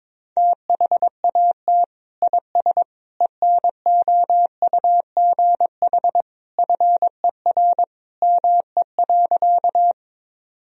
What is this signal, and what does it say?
Morse code, heard as a keyed tone, about 22 wpm, 700 Hz. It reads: THAT IS ENOUGH FER ME.
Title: Sentences from Top 400 Words and Abbreviations 22wpm